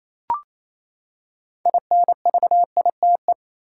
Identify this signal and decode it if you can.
Morse code, heard as a keyed tone, about 28 wpm, 700 Hz. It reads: INVITE
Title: Six Letter Words - 28wpm